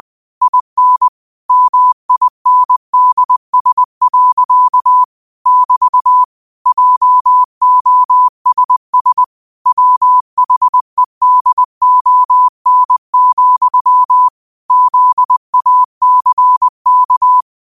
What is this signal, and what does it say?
Morse code, heard as a keyed tone, about 20 wpm, 1000 Hz. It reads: IN MINDS. <BT> JOSS WHEDON, ZACK
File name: qotdfiles / 20wpm